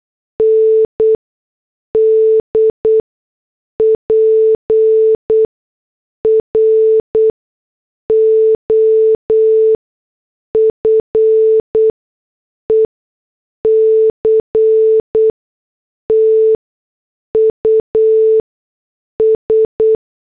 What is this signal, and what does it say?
Morse code, heard as a keyed tone, about 8 wpm, spaced as for 6 wpm, 440 Hz. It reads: ND PROFECTUS
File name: morse / headlines